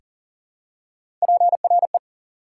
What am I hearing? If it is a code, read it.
Morse code, heard as a keyed tone, about 40 wpm, 700 Hz. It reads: PRE
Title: Prefixes and Suffixes 40wpm